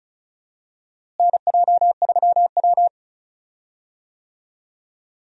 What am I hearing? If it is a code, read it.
Morse code, heard as a keyed tone, about 36 wpm, 700 Hz. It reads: NJ3W
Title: USA Call Signs - 4 Characters - Repeated 2x - 35wpm